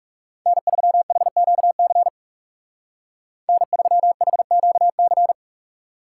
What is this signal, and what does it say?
Morse code, heard as a keyed tone, about 36 wpm, 700 Hz. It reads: N3HQC N3HQC
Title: Speed Racing - USA 1x3 Call Signs 30wpm